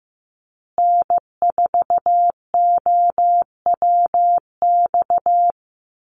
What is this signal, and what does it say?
Morse code, heard as a keyed tone, about 15 wpm, 700 Hz. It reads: N4OWX
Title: USA Call Signs Repeated 3x 15wpm